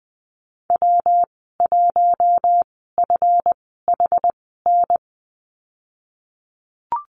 Morse code, heard as a keyed tone, about 20 wpm, spaced as for 14 wpm, 700 Hz. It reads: W1FHN
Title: 147 Lesson 24g - Letter y - Call Signs 20wpm